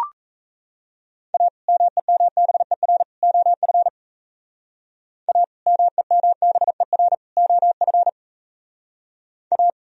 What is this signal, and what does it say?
Morse code, heard as a keyed tone, about 35 wpm, 700 Hz. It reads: A MEMBER OF A MEMBER OF A
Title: Speed-Racing - Sets of 3 Words 28wpm